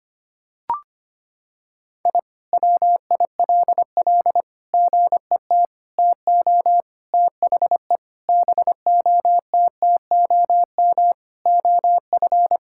Morse code, heard as a keyed tone, about 25 wpm, 700 Hz. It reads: I WILL GET TO THE BOTTOM OF
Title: Sentences from Top 1000 Words - Review Part 6 - 25wpm